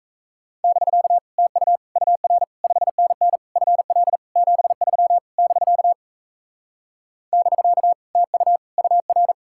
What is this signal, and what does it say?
Morse code, heard as a keyed tone, about 43 wpm, 700 Hz. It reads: <BK> TU UR 5NN FL 73 <BK> <BK> TU UR